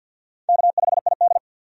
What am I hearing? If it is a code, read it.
Morse code, heard as a keyed tone, about 47 wpm, 700 Hz. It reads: K5ID
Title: USA Call Signs - 4 Characters - 50wpm